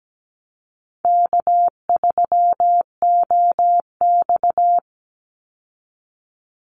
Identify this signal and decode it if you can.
Morse code, heard as a keyed tone, about 17 wpm, 700 Hz. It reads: K3OX
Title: Call Signs Repeated 2x - 17wpm